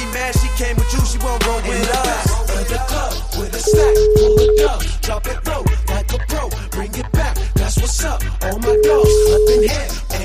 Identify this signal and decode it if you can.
Ringback tone (Europe, 425 Hz), with noise.